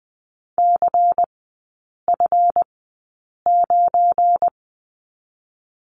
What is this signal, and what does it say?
Morse code, heard as a keyed tone, about 20 wpm, 700 Hz. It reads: C F 9
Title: 198 Lesson 32e - Letter k - Three Characters 20wpm